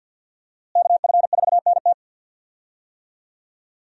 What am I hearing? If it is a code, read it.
Morse code, heard as a keyed tone, about 53 wpm, 700 Hz. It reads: KF4NT
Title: USA Call Signs - 5 Characters - Repeated 3x - 50wpm